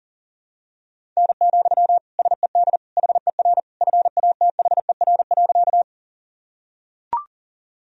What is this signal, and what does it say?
Morse code, heard as a keyed tone, about 40 wpm, 700 Hz. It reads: N, SED HER FATHER.
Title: Sentences from Top 300 Words and Abbreviations 40wpm